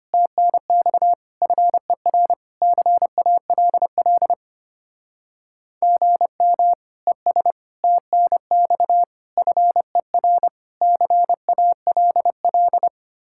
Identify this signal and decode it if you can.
Morse code, heard as a keyed tone, about 31 wpm, 700 Hz. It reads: TNX FER CALL GM ES TNX FER CALL
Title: Speed-Racing Common QSO Phrases Encore - 25wpm